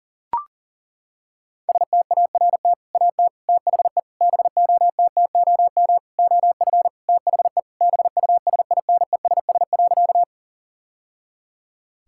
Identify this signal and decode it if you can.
Morse code, heard as a keyed tone, about 40 wpm, 700 Hz. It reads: START AT THE BOTTOM OF THE BUSINESS.
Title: Sentences from Top 1000 Words - Review Part 6 - 40wpm